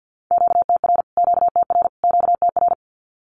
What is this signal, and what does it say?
Morse code, heard as a keyed tone, about 50 wpm, 700 Hz. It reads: QTR QTR QTR
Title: Q-Codes Repeated 3x 50wpm